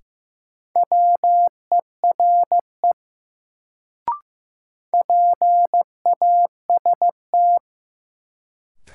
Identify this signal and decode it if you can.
Morse code, heard as a keyed tone, about 15 wpm, 700 Hz. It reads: WERE PAST
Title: Sets of 2 Words - New Word Drill - 15wpm